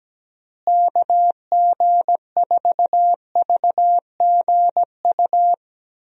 Morse code, heard as a keyed tone, about 17 wpm, 700 Hz. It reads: KG4VGU
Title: USA Call Signs - 6 Characters - Repeated 2x - 17wpm